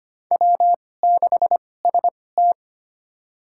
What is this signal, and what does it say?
Morse code, heard as a keyed tone, about 25 wpm, spaced as for 18 wpm, 700 Hz. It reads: W6ST
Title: USA Call Signs - 4 Characters - 2x Character Spacing - 25wpm